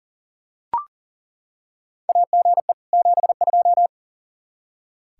Morse code, heard as a keyed tone, about 40 wpm, 700 Hz. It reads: AGE 72
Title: K1USN SST QSO Practice 40wpm